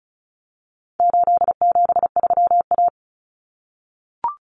Morse code, heard as a keyed tone, about 34 wpm, 700 Hz. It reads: 873A
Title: Common Cut Numbers - 4-Digit 35wpm